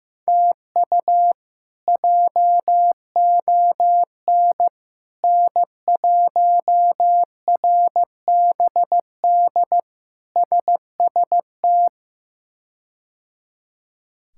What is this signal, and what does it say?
Morse code, heard as a keyed tone, about 15 wpm, 700 Hz. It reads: TU JON N1RBD SST
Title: K1USN SST QSO Practice - 15wpm